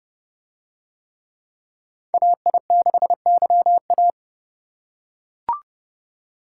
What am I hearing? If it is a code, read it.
Morse code, heard as a keyed tone, about 30 wpm, 700 Hz. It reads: AI6YA